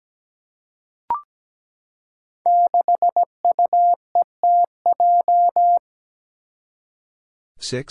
Morse code, heard as a keyed tone, about 17 wpm, 700 Hz. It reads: 6UETJ
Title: Five Challenging Letter-Number Combinations 17wpm